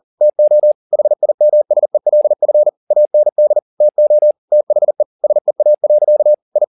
Morse code, heard as a keyed tone, about 40 wpm, 600 Hz. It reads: TO HIMSELF AND TO THE SEA. I
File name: the old man and the sea 2 40wpm